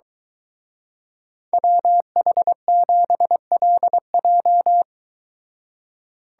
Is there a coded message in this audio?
Yes: Morse code from 1.5 to 4.8 s, heard as a keyed tone, about 23 wpm, 700 Hz. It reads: WH7LJ